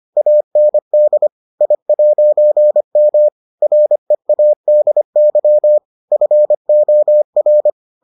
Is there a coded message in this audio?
Yes: Morse code from 0.2 to 7.7 s, heard as a keyed tone, about 25 wpm, 600 Hz. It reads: AND I'M READY FOR